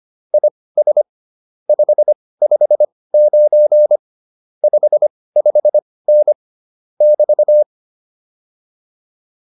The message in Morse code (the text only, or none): IS 559 55N <BT>